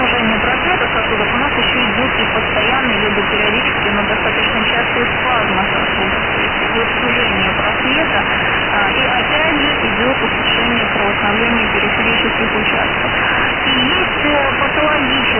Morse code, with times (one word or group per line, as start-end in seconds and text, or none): none